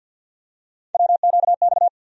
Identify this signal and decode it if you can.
Morse code, heard as a keyed tone, about 47 wpm, 700 Hz. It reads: WQX
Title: USA Call Sign Postfixes - 3 Letters 50wpm